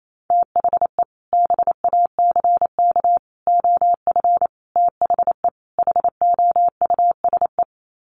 Morse code, heard as a keyed tone, about 28 wpm, 700 Hz. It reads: THE BACK OF THE HOUSE
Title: Sets of 5 Words Encore2 28wpm